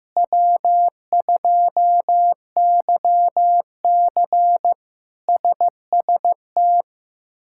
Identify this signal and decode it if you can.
Morse code, heard as a keyed tone, about 15 wpm, 700 Hz. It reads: W2YC SST
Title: K1USN SST QSO Practice - 15wpm